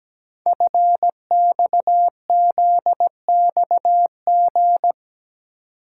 Morse code, heard as a keyed tone, about 17 wpm, 700 Hz. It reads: FXZXG
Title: Code Groups - 5 Letters - 17wpm